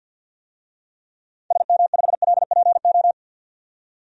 Morse code, heard as a keyed tone, about 47 wpm, 700 Hz. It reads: SM5LPO
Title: DXCC-100 Call Signs - 6 Characters - 50wpm